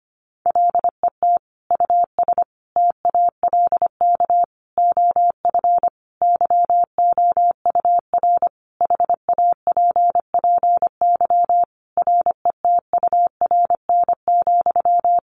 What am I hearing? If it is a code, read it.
Morse code, heard as a keyed tone, about 25 wpm, 700 Hz. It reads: LET US TALK OF YOUR HAPPY RETURN,
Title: Sentences from Top 1000 Words 25wpm